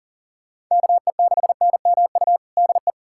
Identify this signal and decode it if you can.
Morse code, heard as a keyed tone, about 40 wpm, 700 Hz. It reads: KE6NMU DE
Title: USA Callsign DE Callsign 40wpm